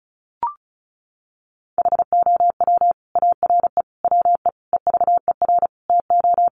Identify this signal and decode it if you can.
Morse code, heard as a keyed tone, about 34 wpm, 700 Hz. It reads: HOW ARE WE EVER TO
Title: Sentences from Top 1500 Words 35wpm